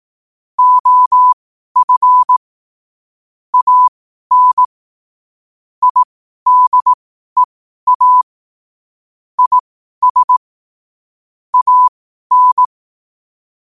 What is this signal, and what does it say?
Morse code, heard as a keyed tone, about 18 wpm, spaced as for 12 wpm, 1000 Hz. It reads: OF AN IDEA IS AN
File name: qotdfiles / 12wpm